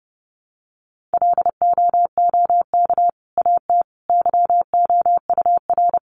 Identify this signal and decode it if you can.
Morse code, heard as a keyed tone, about 30 wpm, 700 Hz. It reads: LOOK AT YOUR
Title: Sentences from Top 1500 Words 30wpm